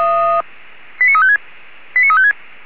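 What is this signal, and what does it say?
Five-tone selective call (ZVEI), heard as quick runs of five tones, tones of about 70 ms. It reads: 89356 89356
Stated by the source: ZVEI five-tone selective call received via SDR.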